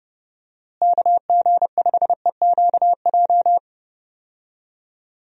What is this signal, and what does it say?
Morse code, heard as a keyed tone, about 32 wpm, 700 Hz. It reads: KG5EQJ
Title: USA Call Signs - 6 Characters - Repeated 3x - 30wpm